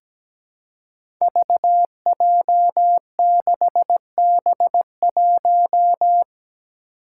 Morse code, heard as a keyed tone, about 17 wpm, 700 Hz. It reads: VJ6B1
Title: Five Challenging Letter-Number Combinations 17wpm